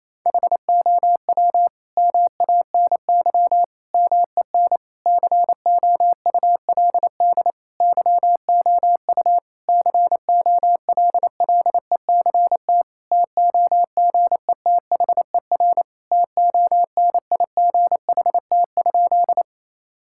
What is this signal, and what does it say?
Morse code, heard as a keyed tone, about 28 wpm, 700 Hz. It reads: HOW MANY MEN COULD YOU COLLECT TOGETHER TONIGHT?
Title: Sentences from Top 1600 Words 28wpm